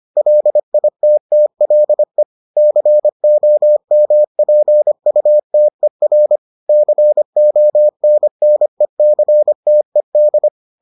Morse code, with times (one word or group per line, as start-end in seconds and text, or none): 0.2-2.2 LITTLE
2.6-6.4 COMPUTER
6.7-10.5 CONNECTED